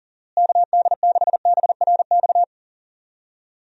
Morse code, heard as a keyed tone, about 40 wpm, 700 Hz. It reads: KD6BRX
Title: USA Call Signs - 6 Characters - Repeated 3x - 40wpm